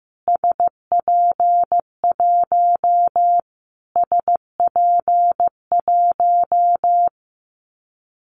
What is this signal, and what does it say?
Morse code, heard as a keyed tone, about 15 wpm, 700 Hz. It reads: SP1 SP1
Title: Callsign Prefixes for Top 25 DXCC - Repeated 3x - 15wpm